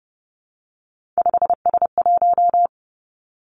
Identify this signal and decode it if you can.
Morse code, heard as a keyed tone, about 30 wpm, 700 Hz. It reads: 5S1
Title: Three Challenging Letter-Number Combinations 30wpm